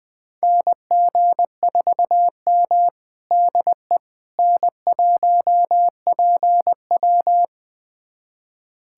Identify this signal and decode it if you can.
Morse code, heard as a keyed tone, about 21 wpm, 700 Hz. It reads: NG4M DE N1PW
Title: USA Callsign DE Callsign - 4 Characters - 20wpm